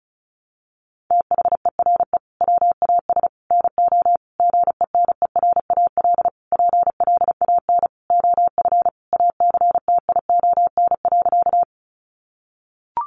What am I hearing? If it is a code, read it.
Morse code, heard as a keyed tone, about 34 wpm, 700 Hz. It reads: THERE WAS NO GENERAL PLAN OF ACTION.
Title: Sentences from Top 700 Words 35wpm